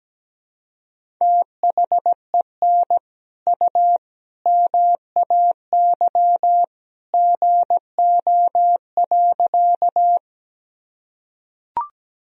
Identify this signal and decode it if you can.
Morse code, heard as a keyed tone, about 17 wpm, 700 Hz. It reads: THEN U MAY GO.
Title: Sentences from Top 100 Words and Abbreviations 17wpm